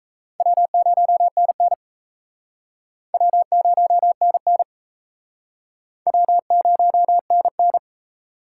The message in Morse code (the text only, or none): W0NN W0NN W0NN